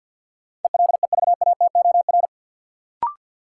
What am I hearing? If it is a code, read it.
Morse code, heard as a keyed tone, about 53 wpm, 700 Hz. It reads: ELEVATOR
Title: Sentences from Top 1600 Words - New Word Drill - 50wpm